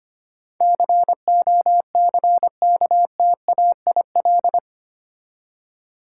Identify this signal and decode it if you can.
Morse code, heard as a keyed tone, about 25 wpm, 700 Hz. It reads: COCKTAIL
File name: 8 Letter Words - First 3 Letters Given - 25wpm